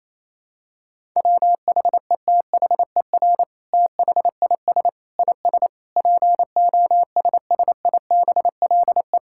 Morse code, heard as a keyed tone, about 28 wpm, 700 Hz. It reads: WHETHER THIS IS POSSIBLE
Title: Sentences from Top 1000 Words - Review Part 5 - 28wpm